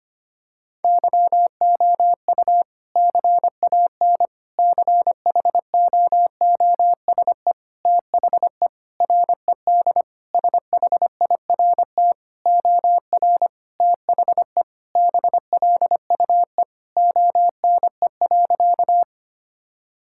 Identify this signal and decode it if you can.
Morse code, heard as a keyed tone, about 25 wpm, 700 Hz. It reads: YOU CAN CHOOSE THE RED SHIRT OR THE BLUE ONE.
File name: practice / 25wpm / Sentences with 2-Word Phrases - 25wpm